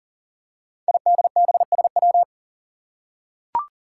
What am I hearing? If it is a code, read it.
Morse code, heard as a keyed tone, about 42 wpm, 700 Hz. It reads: IDBSW
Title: Five Challenging Letter-Number Combinations 40wpm